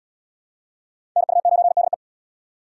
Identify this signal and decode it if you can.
Morse code, heard as a keyed tone, about 75 wpm, 700 Hz. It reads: NU9DE